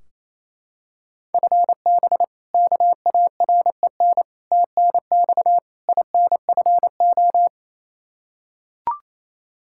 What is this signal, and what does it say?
Morse code, heard as a keyed tone, about 28 wpm, 700 Hz. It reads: FB KAREN TNX INFO
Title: K1USN SST QSO Practice 28wpm